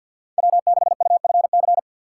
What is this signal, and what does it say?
Morse code, heard as a keyed tone, about 48 wpm, 700 Hz. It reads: W6UFC
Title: Speed Racing - USA 1x3 Call Signs 50wpm